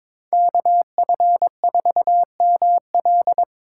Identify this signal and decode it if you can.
Morse code, heard as a keyed tone, about 24 wpm, 700 Hz. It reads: KF4ML
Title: Speed Racing - USA 2x2 Call Signs 22wpm